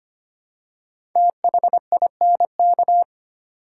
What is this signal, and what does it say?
Morse code, heard as a keyed tone, about 25 wpm, 700 Hz. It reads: THINK
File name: 5 Letter Words - First 3 Letters Given - 25wpm